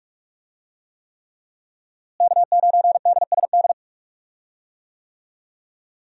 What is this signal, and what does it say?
Morse code, heard as a keyed tone, about 42 wpm, 700 Hz. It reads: K9DSD
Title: USA Call Signs - 5 Characters - Repeated 3x - 45wpm